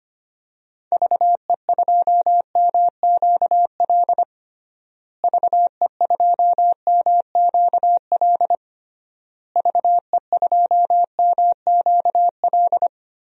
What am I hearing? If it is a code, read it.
Morse code, heard as a keyed tone, about 25 wpm, 700 Hz. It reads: VE2MQL VE2MQL VE2MQL
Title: DXCC-100 Call Signs - 6 Characters - Repeated 3x - 25wpm